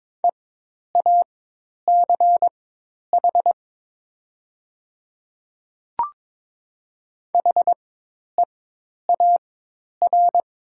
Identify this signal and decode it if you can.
Morse code, heard as a keyed tone, about 22 wpm, spaced as for 10 wpm, 700 Hz. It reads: EACH HEAR
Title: 4x Character Spacing - Top 300 Words - 22wpm